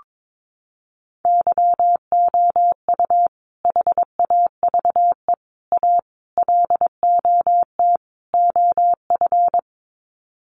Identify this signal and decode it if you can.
Morse code, heard as a keyed tone, about 22 wpm, 700 Hz. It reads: YOU HAVE A LOT OF
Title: Sets of 5 Words Encore 22wpm